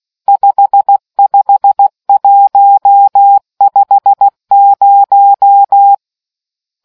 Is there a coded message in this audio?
Yes: Morse code from 0.3 to 5.9 s, heard as a keyed tone, about 16 wpm, 800 Hz. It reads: 55150